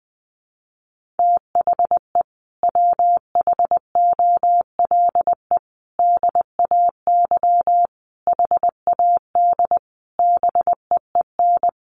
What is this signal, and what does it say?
Morse code, heard as a keyed tone, about 20 wpm, 700 Hz. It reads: THE WHOLE DAY HAD BEEN
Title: Sentences from Top 500 Words 20wpm